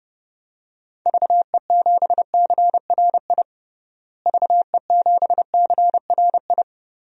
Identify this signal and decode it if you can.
Morse code, heard as a keyed tone, about 30 wpm, 700 Hz. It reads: VE7CRI VE7CRI
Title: DXCC-100 Call Signs - 6 Characters - Repeated 3x - 30wpm